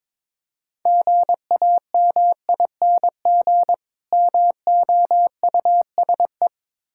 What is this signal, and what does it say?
Morse code, heard as a keyed tone, about 22 wpm, 700 Hz. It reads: GAMING MOUSE
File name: Difficult Words with Hints - 22wpm